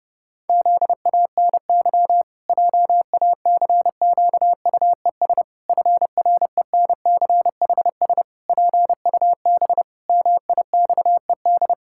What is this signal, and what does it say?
Morse code, heard as a keyed tone, about 28 wpm, 700 Hz. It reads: ZANY JACQUES FRENCHS PUB MIXED
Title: Pangram Challenge v2 - 30wpm